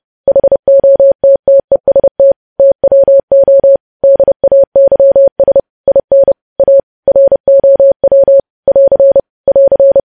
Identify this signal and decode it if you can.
Morse code, heard as a keyed tone, about 30 wpm, 550 Hz. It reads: HOTTEST TWO DAYS IN A ROW <AR> <AR>
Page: news headlines in morse code at 30 wpm